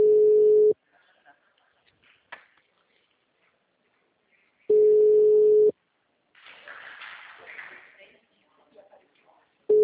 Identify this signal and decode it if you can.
Ringback tone (Europe, 425 Hz), with speech.